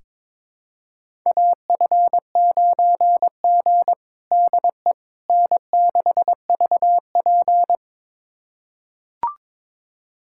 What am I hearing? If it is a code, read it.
Morse code, heard as a keyed tone, about 22 wpm, 700 Hz. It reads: AF9G DE N6VP
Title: USA Callsign DE Callsign - 4 Characters - 22wpm